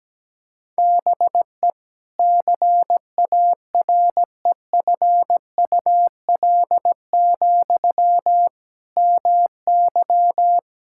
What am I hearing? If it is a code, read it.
Morse code, heard as a keyed tone, about 17 wpm, 700 Hz. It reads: BE CAREFUL, MY
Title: Sentences from Top 1600 Words 17wpm